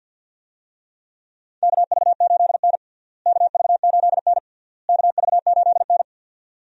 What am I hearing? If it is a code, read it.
Morse code, heard as a keyed tone, about 49 wpm, 700 Hz. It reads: KV8N KV8N KV8N
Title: USA Call Signs - 4 Characters - Repeated 3x - 50wpm